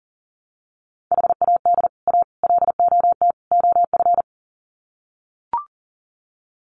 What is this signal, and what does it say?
Morse code, heard as a keyed tone, about 40 wpm, 700 Hz. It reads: HAD A LOT OF
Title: Sets of 4 Words 40wpm